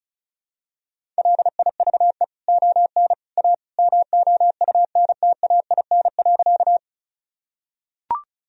Morse code, heard as a keyed tone, about 36 wpm, 700 Hz. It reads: LIVE ON A MOUNTAIN.
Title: Sentences from Top 1000 Words - Review Part 1 - 35wpm